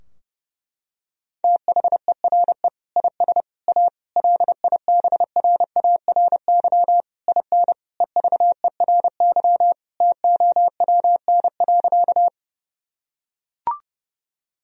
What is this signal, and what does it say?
Morse code, heard as a keyed tone, about 30 wpm, 700 Hz. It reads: THERE IS A LIBRARY IN EVERY TOWN.
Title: Sentences from Top 1000 Words - Review Part 4 - 30wpm